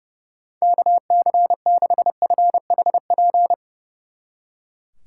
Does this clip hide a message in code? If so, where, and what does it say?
Yes: Morse code from 0.6 to 3.5 s, heard as a keyed tone, about 38 wpm, 700 Hz. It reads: KC6FHP